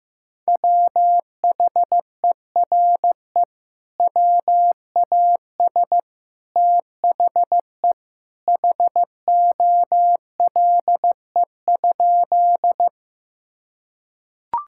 Morse code, heard as a keyed tone, about 16 wpm, 700 Hz. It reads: WHERE WAS THE HOLE?
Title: Sentences from Top 1600 Words 15wpm